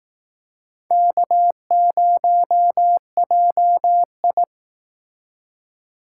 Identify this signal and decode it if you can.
Morse code, heard as a keyed tone, about 20 wpm, 700 Hz. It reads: K0JI